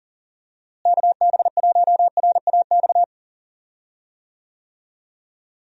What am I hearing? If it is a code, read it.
Morse code, heard as a keyed tone, about 39 wpm, 700 Hz. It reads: KB1RAX
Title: USA Call Signs - 6 Characters - Repeated 2x - 40wpm